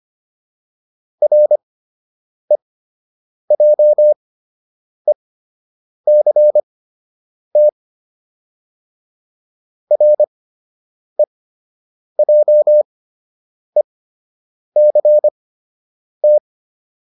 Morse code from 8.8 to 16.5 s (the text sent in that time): REJECT